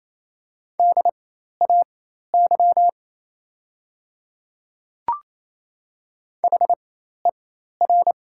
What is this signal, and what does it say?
Morse code, heard as a keyed tone, about 28 wpm, spaced as for 13 wpm, 700 Hz. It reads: DAY HER
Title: 4x Character Spacing - Top 100 Words - 28wpm